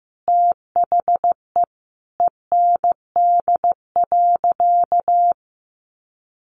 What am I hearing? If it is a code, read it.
Morse code, heard as a keyed tone, about 15 wpm, 700 Hz. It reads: THE END.
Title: Sentences from Top 1300 Words 15wpm